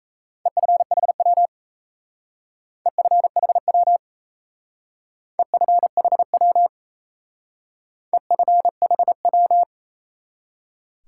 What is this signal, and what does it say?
Morse code, heard as a keyed tone, about 34 wpm, 700 Hz. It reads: EFHW EFHW EFHW EFHW
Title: Speed-Racing Top 180 QSO Elements 28wpm